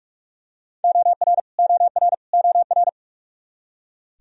Morse code, heard as a keyed tone, about 43 wpm, 700 Hz. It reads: OR OR OR
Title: Top 100 Words Repeated 45wpm